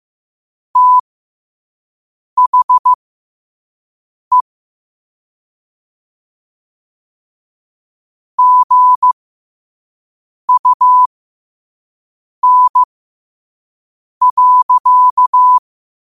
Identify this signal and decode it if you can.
Morse code, heard as a keyed tone, about 15 wpm, spaced as for 5 wpm, 1000 Hz. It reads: THE GUN.
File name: qotdfiles / 5wpm